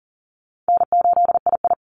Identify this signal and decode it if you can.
Morse code, heard as a keyed tone, about 40 wpm, 700 Hz. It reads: N8II
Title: Easy Call Signs - 40wpm